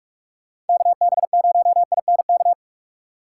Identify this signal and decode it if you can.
Morse code, heard as a keyed tone, about 42 wpm, 700 Hz. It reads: KB0INK
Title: USA Call Signs - 4x Character Spacing - 45wpm